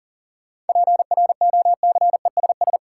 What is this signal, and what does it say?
Morse code, heard as a keyed tone, about 40 wpm, 700 Hz. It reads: PROCESS
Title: Sentences from Top 1000 Words - Review Part 3 - New Word Drill - 40wpm